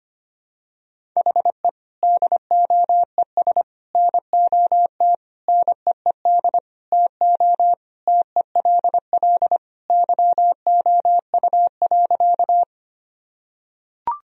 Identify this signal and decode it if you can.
Morse code, heard as a keyed tone, about 25 wpm, 700 Hz. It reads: HE DOES NOT NEED TO TELL YOU.
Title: Sentences from Top 1000 Words - Review Part 1 - 25wpm